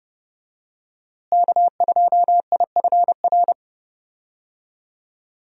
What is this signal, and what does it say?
Morse code, heard as a keyed tone, about 31 wpm, 700 Hz. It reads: K2IFR
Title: USA Call Signs - 5 Characters - Repeated 3x - 30wpm